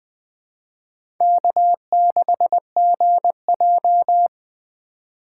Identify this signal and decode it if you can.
Morse code, heard as a keyed tone, about 20 wpm, 700 Hz. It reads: K6GJ
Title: USA Call Signs - 4x Character Spacing - 20wpm